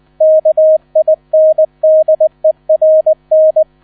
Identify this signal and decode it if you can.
Morse code, heard as a keyed tone, about 20 wpm, 630 Hz. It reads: KINDERN
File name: 000 Vorwort 20wpm